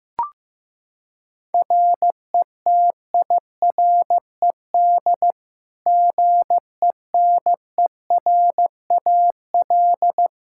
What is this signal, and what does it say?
Morse code, heard as a keyed tone, about 15 wpm, 700 Hz. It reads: RETIRED GENERAL
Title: Practice QSO - Exchange 3 - More Info - 15wpm